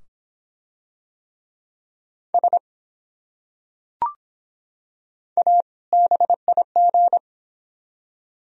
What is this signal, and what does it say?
Morse code, heard as a keyed tone, about 24 wpm, 700 Hz. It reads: S A BIG